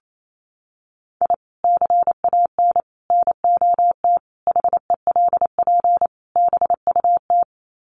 Morse code, heard as a keyed tone, about 28 wpm, 700 Hz. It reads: I CAN NOT HELP BUT
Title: Sets of 5 Words Encore2 28wpm